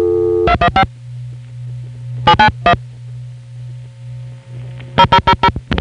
DTMF, heard as telephone touch tones, tones of about 70 ms. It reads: 215 791 0000